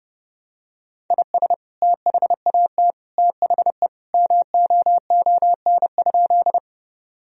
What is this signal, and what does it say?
Morse code, heard as a keyed tone, about 30 wpm, 700 Hz. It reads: IS THAT THE MOON?
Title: Sentences from Top 1000 Words - Review Part 3 - 30wpm